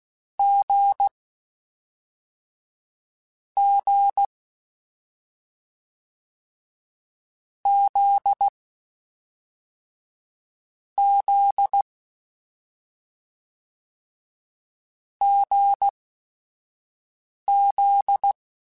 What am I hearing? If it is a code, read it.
Morse code, heard as a keyed tone, about 16 wpm, spaced as for 5 wpm, 800 Hz. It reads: G G Z Z GZ
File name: Short Runs / Short Runs-mp3s / awj1ndb6gz7 @5